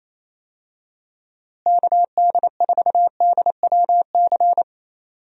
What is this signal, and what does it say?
Morse code, heard as a keyed tone, about 28 wpm, 700 Hz. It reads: KD4DWC
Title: USA Call Signs - 6 Characters - Repeated 3x - 28wpm